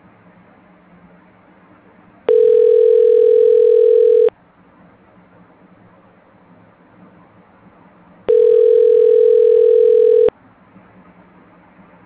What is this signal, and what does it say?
Ringback tone (North America), 440 + 480 Hz.